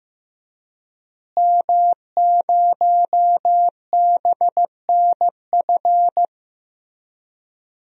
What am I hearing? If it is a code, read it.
Morse code, heard as a keyed tone, about 15 wpm, 700 Hz. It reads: M0BNF
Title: DXCC-100 Call Signs - 5 Characters - Repeated 2x - 15wpm